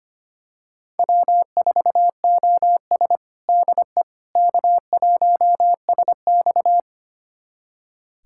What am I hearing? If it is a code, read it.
Morse code, heard as a keyed tone, about 25 wpm, 700 Hz. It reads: W4OS DE K1SX
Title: USA Callsign DE Callsign - 4 Characters - 25wpm